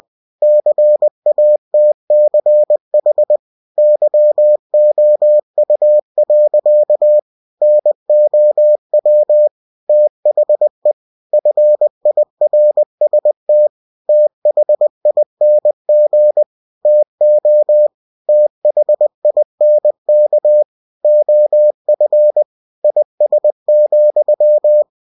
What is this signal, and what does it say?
Morse code, heard as a keyed tone, about 20 wpm, 600 Hz. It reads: CATCH YOU. NOW THE FIRST THING TO THINK OF IS,
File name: cw-books / winnie-the-pooh / winnie-20wpm-page-0037